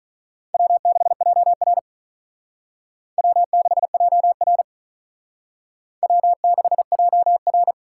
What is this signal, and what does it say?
Morse code, heard as a keyed tone, about 40 wpm, 700 Hz. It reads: W6JR W6JR W6JR
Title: Speed Racing - USA 1x2 Call Signs 35wpm